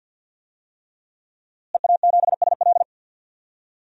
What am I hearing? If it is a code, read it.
Morse code, heard as a keyed tone, about 52 wpm, 700 Hz. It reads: EA7SL
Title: DXCC-100 Call Signs - 5 Characters - Repeated 3x - 50wpm